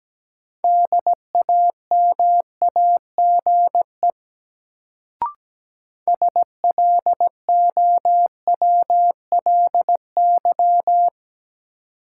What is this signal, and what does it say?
Morse code, heard as a keyed tone, about 17 wpm, 700 Hz. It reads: DAMAGE SLOWLY
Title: Sentences from Top 1100 Words - New Word Drill - 17wpm